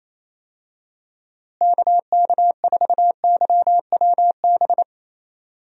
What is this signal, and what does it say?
Morse code, heard as a keyed tone, about 27 wpm, 700 Hz. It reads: KK4YWB
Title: USA Call Signs - 6 Characters - Repeated 3x - 28wpm